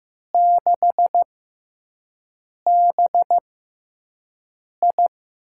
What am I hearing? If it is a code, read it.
Morse code, heard as a keyed tone, about 15 wpm, spaced as for 5 wpm, 700 Hz. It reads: 6BI
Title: USA Call Signs - 4 Characters - 6x Character Spacing - 15wpm